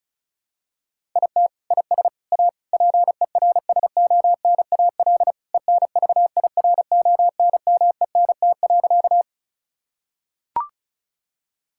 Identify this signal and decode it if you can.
Morse code, heard as a keyed tone, about 36 wpm, 700 Hz. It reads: IT IS A PERSONAL ENVIRONMENT.
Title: Sentences from Top 1000 Words - Review Part 5 - 35wpm